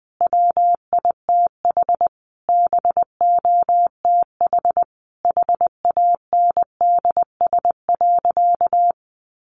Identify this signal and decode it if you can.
Morse code, heard as a keyed tone, about 20 wpm, 700 Hz. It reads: WITH BOTH HANDS.
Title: Easy Phrases 20wpm